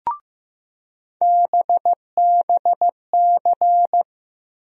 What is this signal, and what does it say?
Morse code, heard as a keyed tone, about 15 wpm, 700 Hz. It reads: BBC
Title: USA Call Sign Postfixes - 3 Letters 15wpm